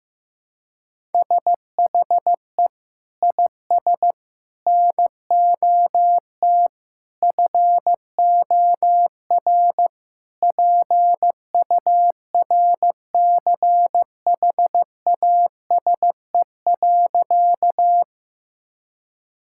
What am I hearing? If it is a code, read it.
Morse code, heard as a keyed tone, about 15 wpm, 700 Hz. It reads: SHE IS NOT FOR PURCHASE.
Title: Sentences from Top 1000 Words - Review Part 5 - 15wpm